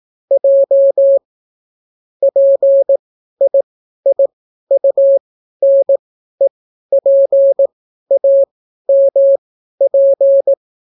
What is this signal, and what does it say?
Morse code, heard as a keyed tone, about 18 wpm, spaced as for 12 wpm, 550 Hz. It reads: J PIIUNEPAMP